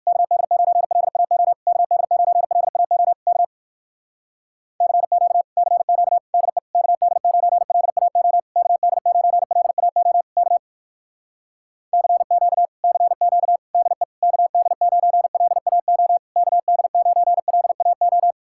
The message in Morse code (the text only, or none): KD9LAO KD9LAO K CQ CQ DE KD9LAO KD9LAO K CQ CQ DE KD9LAO KD9LAO